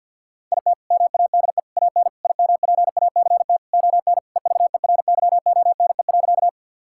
Morse code, heard as a keyed tone, about 50 wpm, 700 Hz. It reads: IT MADE AN IMPACT ON EVERYONE.
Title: Sentences from Top 1000 Words - Review Part 6 - 50wpm